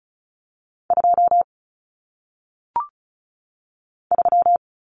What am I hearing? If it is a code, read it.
Morse code, heard as a keyed tone, about 36 wpm, 700 Hz. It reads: 2 3
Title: Common Cut Numbers - 1-Digit 35wpm